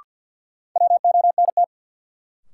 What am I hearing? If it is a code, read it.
Morse code, heard as a keyed tone, about 53 wpm, 700 Hz. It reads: WONT
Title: Sentences from Top 1000 Words - Review Part 6 - New Word Drill - 50wpm